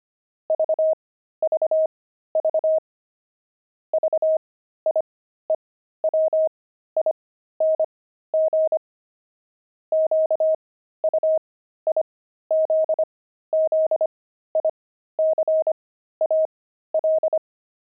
Morse code, heard as a keyed tone, about 25 wpm, spaced as for 13 wpm, 630 Hz. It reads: VVV VIEWING QUIZZICAL
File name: pangram-fs-13-cs-25-jocks